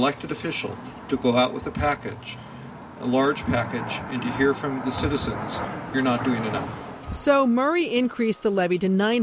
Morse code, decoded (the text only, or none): none